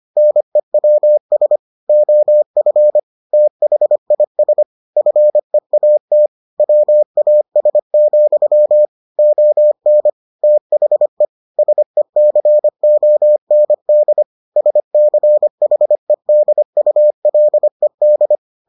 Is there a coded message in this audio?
Yes: Morse code from 0.2 to 18.4 s, heard as a keyed tone, about 25 wpm, 600 Hz. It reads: NEWS OF THIS FEAT WAS, ON THE SECOND SCHEDULED